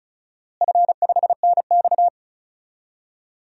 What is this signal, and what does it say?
Morse code, heard as a keyed tone, about 34 wpm, 700 Hz. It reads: F5NX
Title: DXCC-100 Call Signs - 4 Characters - Repeated 2x - 35wpm